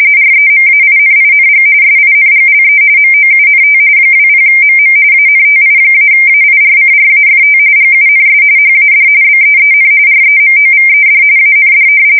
RTTY, about 45.45 baud. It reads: REMAINING SINGLE MOUNT 20MM ANTI-AIRCRAFT GUNS HAD BEEN REMOVED AND
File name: fsk170 / battleshipnj